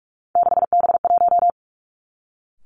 Morse code, heard as a keyed tone, about 46 wpm, 700 Hz. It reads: 6B1
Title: Three Challenging Letter-Number Combinations 45wpm